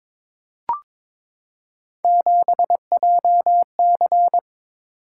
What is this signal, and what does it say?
Morse code, heard as a keyed tone, about 23 wpm, 700 Hz. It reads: 7JC
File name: practice / 22wpm / Three Letter-Number Combinations - 22wpm